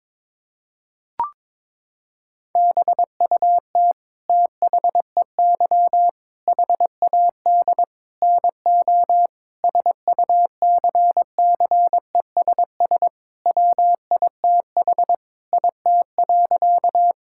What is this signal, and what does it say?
Morse code, heard as a keyed tone, about 22 wpm, 700 Hz. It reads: BUT THEY HAD NO SUCCESS WITH IT.
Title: Sentences from Top 1000 Words - Review Part 6 - 22wpm